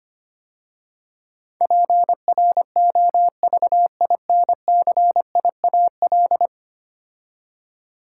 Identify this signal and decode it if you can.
Morse code, heard as a keyed tone, about 25 wpm, 700 Hz. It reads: PROVINCIAL
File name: 10 Letter Words - First 3 Letters Given - 25wpm